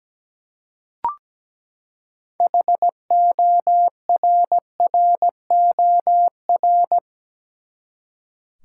Morse code, heard as a keyed tone, about 17 wpm, 700 Hz. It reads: HORROR
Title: Six Letter Words - 17wpm